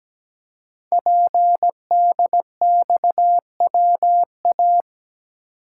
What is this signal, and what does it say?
Morse code, heard as a keyed tone, about 17 wpm, 700 Hz. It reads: PDXWA
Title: Code Groups - 5 Letter:Numbers - 17wpm